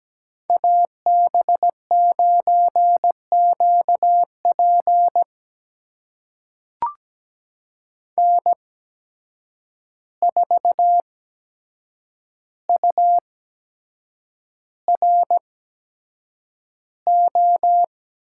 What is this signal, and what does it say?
Morse code, heard as a keyed tone, about 17 wpm, spaced as for 7 wpm, 700 Hz. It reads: AB9QP N4URO
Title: USA Call Signs - 5 Characters - 8x Character Spacing - 17wpm